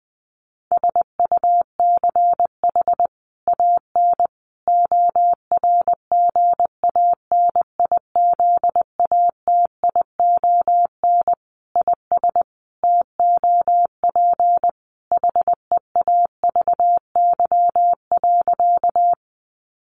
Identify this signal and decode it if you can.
Morse code, heard as a keyed tone, about 19 wpm, 700 Hz. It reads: SUCH AN ORGANIZATION IS TOP HEAVY.
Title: Sentences from Top 1100 Words 20wpm